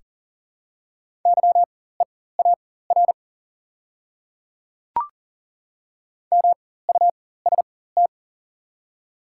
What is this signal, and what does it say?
Morse code, heard as a keyed tone, about 40 wpm, spaced as for 19 wpm, 700 Hz. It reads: YEAR MUST
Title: 4x Character Spacing - Top 200 Words - 40wpm